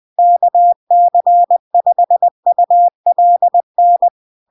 Morse code, heard as a keyed tone, about 20 wpm, 700 Hz. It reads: KC5ULN
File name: LARK-FD-2022-Log-in-CW-20wpm-700-Hz